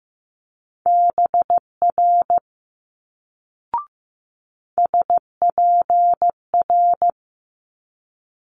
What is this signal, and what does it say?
Morse code, heard as a keyed tone, about 15 wpm, 700 Hz. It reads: BR SPR
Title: Multi-letter Phonemes Quick Fire 15wpm